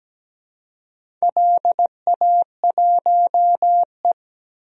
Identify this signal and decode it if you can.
Morse code, heard as a keyed tone, about 17 wpm, 700 Hz. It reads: LA1E
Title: DXCC-100 Call Signs - 4 Characters - 17wpm